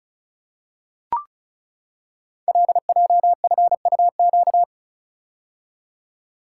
Morse code, heard as a keyed tone, about 36 wpm, 690 Hz. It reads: LJFUQ
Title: Code Groups - 5 Letters - 35wpm